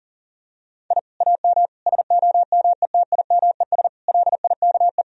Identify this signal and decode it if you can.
Morse code, heard as a keyed tone, about 40 wpm, 700 Hz. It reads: I AM SOMETIMES LIKE
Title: Sentences from Top 1000 Words - Review Part 3 - 40wpm